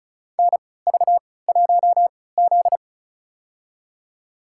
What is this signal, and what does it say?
Morse code, heard as a keyed tone, about 36 wpm, spaced as for 20 wpm, 700 Hz. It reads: NV1Z
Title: USA Call Signs - 3x Character Spacing - 35wpm